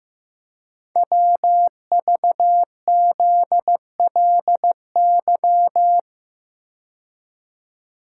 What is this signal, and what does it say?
Morse code, heard as a keyed tone, about 15 wpm, 690 Hz. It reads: WVZLY
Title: Code Groups - 5 Letters - 15wpm